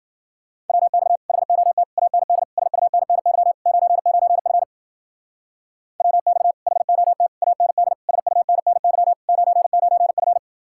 Morse code, heard as a keyed tone, about 49 wpm, 700 Hz. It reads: WX HOT AND SUNNY 99F WX HOT AND SUNNY 99F
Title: Speed-Racing Common QSO Phrases - 40wpm